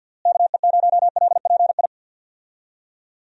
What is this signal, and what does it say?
Morse code, heard as a keyed tone, about 52 wpm, 700 Hz. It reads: KE0LPI
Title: USA Call Signs - 6 Characters - Repeated 3x - 50wpm